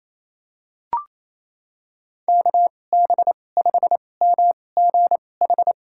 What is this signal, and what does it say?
Morse code, heard as a keyed tone, about 27 wpm, spaced as for 20 wpm, 700 Hz. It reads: KB5MGH
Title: USA Call Signs - 6 Characters - 2x Character Spacing - 28wpm